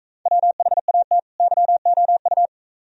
Morse code, heard as a keyed tone, about 40 wpm, 700 Hz. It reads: WHAT YOU